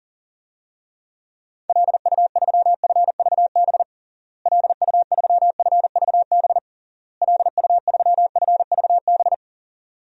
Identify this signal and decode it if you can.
Morse code, heard as a keyed tone, about 40 wpm, 700 Hz. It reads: LU3FVB LU3FVB LU3FVB
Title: DXCC-100 Call Signs - 6 Characters - Repeated 3x - 40wpm